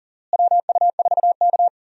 Morse code, heard as a keyed tone, about 40 wpm, 700 Hz. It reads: WU4K
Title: USA Call Signs - 4 Characters - Repeated 3x - 40wpm